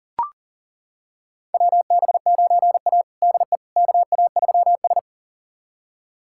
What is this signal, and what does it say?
Morse code, heard as a keyed tone, about 40 wpm, 700 Hz. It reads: WB9A DE KA3S
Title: USA Callsign DE Callsign - 4 Characters - 40wpm